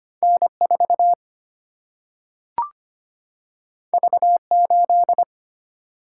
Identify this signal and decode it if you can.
Morse code, heard as a keyed tone, about 25 wpm, 700 Hz. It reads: N 4 V 8
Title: Single-Letter Single-Number 25wpm